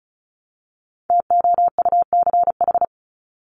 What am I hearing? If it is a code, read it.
Morse code, heard as a keyed tone, about 34 wpm, 700 Hz. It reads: TOUCH
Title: Top 1001-1250 Words 35wpm